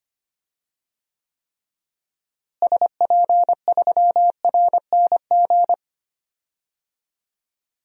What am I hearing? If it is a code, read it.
Morse code, heard as a keyed tone, about 25 wpm, 700 Hz. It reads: SP3RNG
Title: DXCC-100 Call Signs - 6 Characters - Repeated 3x - 25wpm